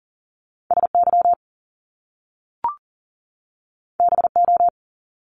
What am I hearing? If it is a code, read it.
Morse code, heard as a keyed tone, about 40 wpm, 700 Hz. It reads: SY BO
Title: Top Two Letter Combinations in Words 40wpm